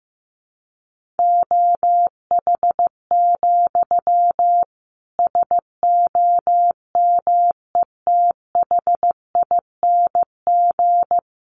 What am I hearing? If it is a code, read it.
Morse code, heard as a keyed tone, about 15 wpm, 700 Hz. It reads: OH, SOMETHING
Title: Sentences from Top 1500 Words 15wpm